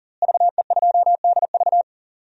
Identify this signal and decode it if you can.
Morse code, heard as a keyed tone, about 40 wpm, 690 Hz. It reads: VE2DV
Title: DXCC-100 Call Signs - 5 Characters - Repeated 2x - 40wpm